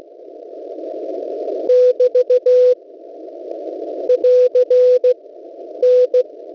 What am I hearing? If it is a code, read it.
Morse code, heard as a keyed tone, about 15 wpm, 500 Hz. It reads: <BT> <AR> N
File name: WebSDR records / w7pua-2.ddns.net 6477.50 cw slow